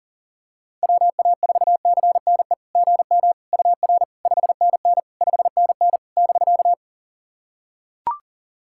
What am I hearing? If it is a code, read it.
Morse code, heard as a keyed tone, about 40 wpm, 690 Hz. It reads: WA4CNE GM UR 5NN 5NN <BK>
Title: SOTA QSO Practice 40wpm